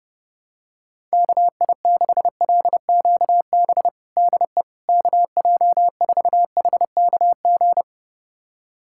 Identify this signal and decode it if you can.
Morse code, heard as a keyed tone, about 28 wpm, 700 Hz. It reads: KI6LQB DE KJ4HKG
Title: USA Callsign DE Callsign - 6 Characters - 30wpm